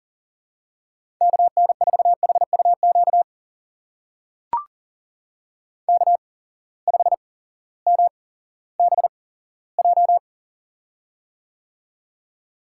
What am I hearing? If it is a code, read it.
Morse code, heard as a keyed tone, about 40 wpm, spaced as for 18 wpm, 700 Hz. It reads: KN4HUQ K5MBJ
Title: USA Call Signs - 8x Character Spacing - 40wpm